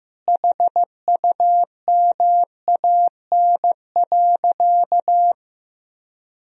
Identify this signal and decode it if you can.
Morse code, heard as a keyed tone, about 15 wpm, 700 Hz. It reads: HUMAN.
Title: Sentences from Top 1000 Words - Review Part 3 - 15wpm